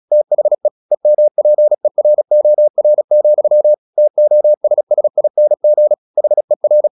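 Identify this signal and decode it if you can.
Morse code, heard as a keyed tone, about 36 wpm, 600 Hz. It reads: THE EMPEROR, TOSSING HER
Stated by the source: Morse practice at 36 wpm, text of Ken Follett's novel Never, from CW med Gnister.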